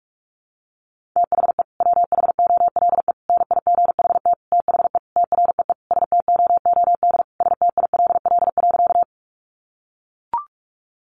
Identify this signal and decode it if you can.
Morse code, heard as a keyed tone, about 44 wpm, 700 Hz. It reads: THE WHOLE NIGHT THE TREE STOOD STILL.
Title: Sentences from Top 500 Words 45wpm